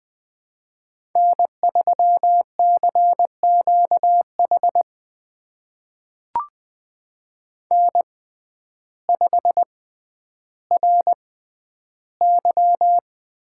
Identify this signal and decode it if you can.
Morse code, heard as a keyed tone, about 21 wpm, spaced as for 11 wpm, 700 Hz. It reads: N3CQH N5RY
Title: USA Call Signs - 5 Characters - 6x Character Spacing - 20wpm